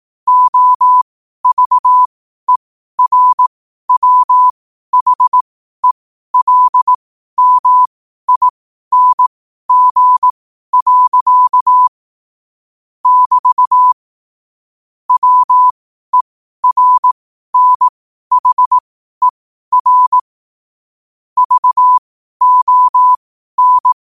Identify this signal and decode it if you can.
Morse code, heard as a keyed tone, about 18 wpm, spaced as for 12 wpm, 1000 Hz. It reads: OVERWHELMING. <BT> WERNHER VON
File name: qotdfiles / 12wpm